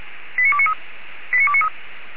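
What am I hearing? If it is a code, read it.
Five-tone selective call (ZVEI), heard as quick runs of five tones, tones of about 70 ms. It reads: 89293 89293
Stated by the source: ZVEI five-tone selective call received by SDR, recorded on the audio server of Feuerwehr Orlamünde.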